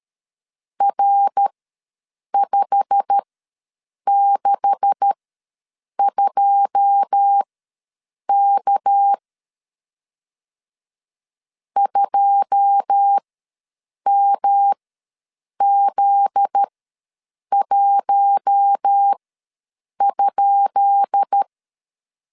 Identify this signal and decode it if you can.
Morse code, heard as a keyed tone, about 13 wpm, spaced as for 7 wpm, 800 Hz. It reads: R562K 2MZ1?
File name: LEZIONI / 10 SIGNS / 10D COMA ws 12wpm 3618